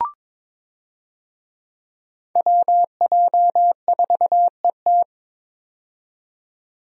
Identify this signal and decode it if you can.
Morse code, heard as a keyed tone, about 22 wpm, 700 Hz. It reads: WJ4ET
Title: USA Call Signs - 5 Characters - 22wpm